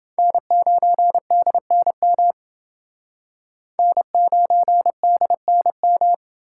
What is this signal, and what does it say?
Morse code, heard as a keyed tone, about 29 wpm, 700 Hz. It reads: N9DNM N9DNM